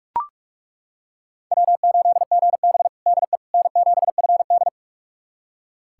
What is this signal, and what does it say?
Morse code, heard as a keyed tone, about 42 wpm, 700 Hz. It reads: W8GB DE N7FD
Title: USA Callsign DE Callsign - 4 Characters - 45wpm